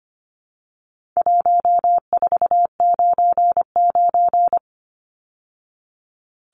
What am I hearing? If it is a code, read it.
Morse code, heard as a keyed tone, about 25 wpm, 700 Hz. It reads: 1499
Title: Numbers - 4-Digit 25wpm